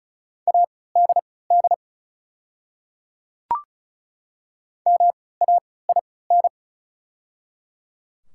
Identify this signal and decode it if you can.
Morse code, heard as a keyed tone, about 36 wpm, spaced as for 20 wpm, 700 Hz. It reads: ADD MAIN
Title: 3x Character Spacing - Top 300 Words - 35wpm